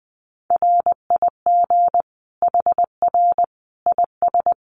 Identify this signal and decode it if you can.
Morse code, heard as a keyed tone, about 19 wpm, 700 Hz. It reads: RIG HR IS
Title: Common QSO Phrases 20wpm